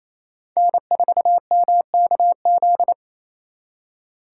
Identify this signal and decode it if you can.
Morse code, heard as a keyed tone, about 36 wpm, 700 Hz. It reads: N4MKZ